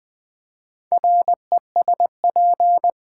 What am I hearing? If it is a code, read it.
Morse code, heard as a keyed tone, about 20 wpm, 700 Hz. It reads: RESP
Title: Build A Word - 4 Letters At A Time - 11-Letter Words - 20wpm